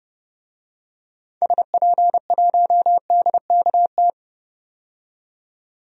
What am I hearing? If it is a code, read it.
Morse code, heard as a keyed tone, about 30 wpm, 700 Hz. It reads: SP1DKT
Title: DXCC-100 Call Signs - 6 Characters - 30wpm